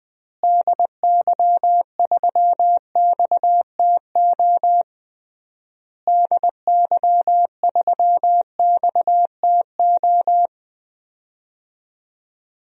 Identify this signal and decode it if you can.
Morse code, heard as a keyed tone, about 21 wpm, 700 Hz. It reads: DY3XTO DY3XTO
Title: DXCC-100 Call Signs - 6 Characters - Repeated 3x - 20wpm